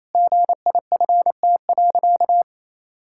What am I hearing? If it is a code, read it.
Morse code, heard as a keyed tone, about 28 wpm, 700 Hz. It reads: GIFT.
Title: Sentences from Top 1000 Words - Review Part 6 - 28wpm